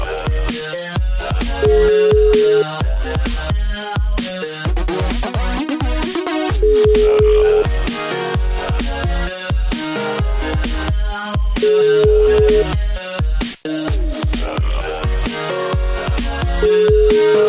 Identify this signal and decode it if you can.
Ringback tone (Europe, 425 Hz), with noise.